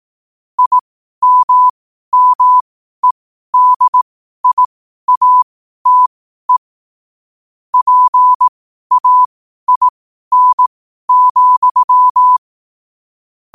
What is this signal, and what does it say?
Morse code, heard as a keyed tone, about 18 wpm, spaced as for 12 wpm, 1000 Hz. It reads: IMMEDIATE PAIN,
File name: 12wpm